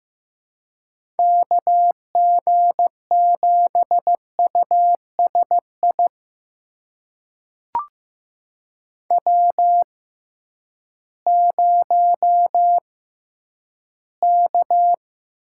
Morse code, heard as a keyed tone, about 15 wpm, spaced as for 10 wpm, 700 Hz. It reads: KG7USI W0K
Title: USA Call Signs - 6x Character Spacing - 15wpm